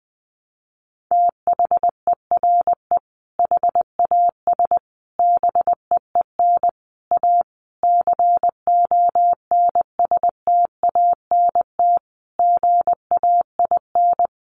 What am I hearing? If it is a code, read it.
Morse code, heard as a keyed tone, about 20 wpm, 700 Hz. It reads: THERE HAS BEEN A CONSTANT GAIN
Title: Sentences from Top 1500 Words 20wpm